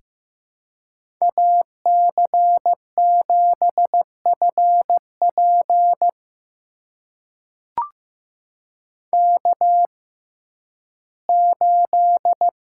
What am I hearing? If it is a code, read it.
Morse code, heard as a keyed tone, about 15 wpm, spaced as for 11 wpm, 700 Hz. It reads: AC7FP K8
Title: USA Call Signs - 5 Characters - 6x Character Spacing - 15wpm